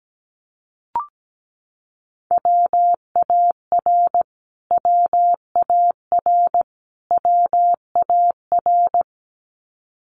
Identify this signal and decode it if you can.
Morse code, heard as a keyed tone, about 17 wpm, 700 Hz. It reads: WAR WAR WAR
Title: Top 500 Words Repeated 17wpm